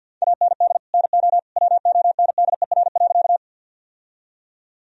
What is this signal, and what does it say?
Morse code, heard as a keyed tone, about 50 wpm, 700 Hz. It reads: AND NO WONDER.
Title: Sentences from Top 1000 Words - Review Part 3 - 50wpm